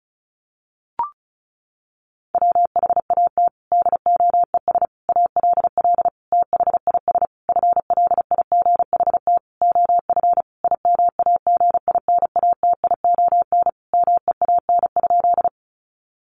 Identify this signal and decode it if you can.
Morse code, heard as a keyed tone, about 34 wpm, 700 Hz. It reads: WHAT DOES ALL THIS FLIGHT OF IMAGINATION MEAN?
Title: Sentences from Top 1300 Words 35wpm